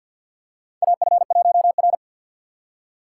Morse code, heard as a keyed tone, about 49 wpm, 700 Hz. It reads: AF1R
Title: USA Call Signs - 4x Character Spacing - 50wpm